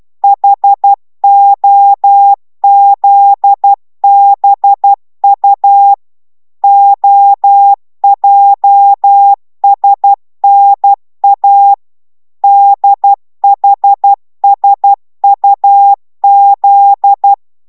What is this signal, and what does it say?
Morse code, heard as a keyed tone, about 12 wpm, 800 Hz. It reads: HOZBU OJSNA DHSUZ